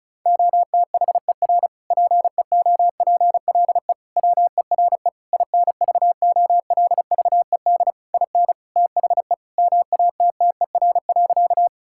Morse code, heard as a keyed tone, about 34 wpm, 700 Hz. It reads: OTHER PEOPLE WERE INVOLVED IN THE MATTER.
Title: Sentences from Top 1000 Words - Review Part 5 - 35wpm